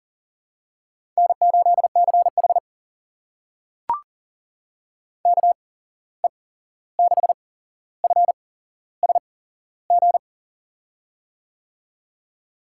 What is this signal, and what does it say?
Morse code, heard as a keyed tone, about 40 wpm, spaced as for 15 wpm, 700 Hz. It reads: N8CH KE6FSG
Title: USA Call Signs - 8x Character Spacing - 40wpm